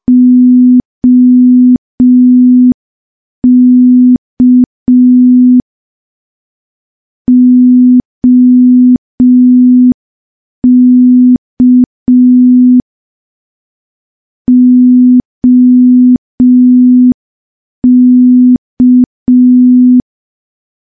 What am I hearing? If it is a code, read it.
Morse code, heard as a keyed tone, about 5 wpm, 250 Hz. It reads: OK OK OK
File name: morse / morse-training-05WPM